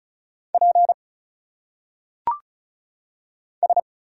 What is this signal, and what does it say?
Morse code, heard as a keyed tone, about 35 wpm, 700 Hz. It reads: P S
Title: Single Letters - Phonetic Pronunciation 35wpm